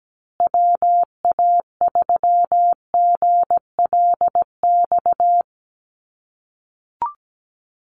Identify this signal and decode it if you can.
Morse code, heard as a keyed tone, about 17 wpm, 700 Hz. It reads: WA3GLX
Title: USA Call Signs Repeated 2x 17wpm